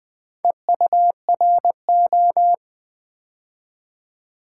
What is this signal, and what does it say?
Morse code, heard as a keyed tone, about 20 wpm, 690 Hz. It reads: EURO
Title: Build A Word - 2 Letters At A Time - 4-Letter Words - 20wpm